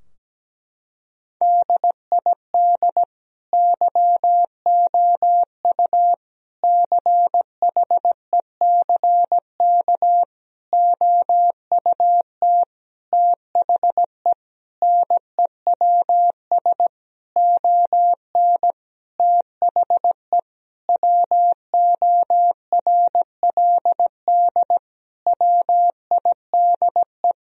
Morse code, heard as a keyed tone, about 17 wpm, 700 Hz. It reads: DID YOU CHECK OUT THE NEWS ON THE WORLD WIDE
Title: Sentences from Top 1000 Words - Review Part 4 - 17wpm